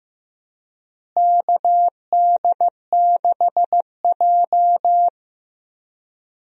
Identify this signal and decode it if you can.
Morse code, heard as a keyed tone, about 15 wpm, 700 Hz. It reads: KD6J
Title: USA Call Signs - 4 Characters - Repeated 3x - 15wpm